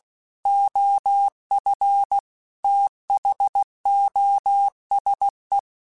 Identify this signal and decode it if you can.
Morse code, heard as a keyed tone, about 16 wpm, 800 Hz. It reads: OF THOSE
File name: Technical Subjects / Technical Subjects-mp3s / OR Gate @15